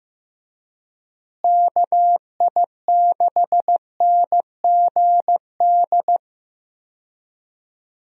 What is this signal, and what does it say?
Morse code, heard as a keyed tone, about 15 wpm, 700 Hz. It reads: KI6NGD
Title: USA Call Signs - 6 Characters - Repeated 3x - 15wpm